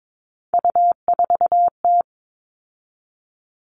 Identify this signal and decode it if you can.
Morse code, heard as a keyed tone, about 22 wpm, 700 Hz. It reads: U4T
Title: All Cut Numbers - 3-Digit 22wpm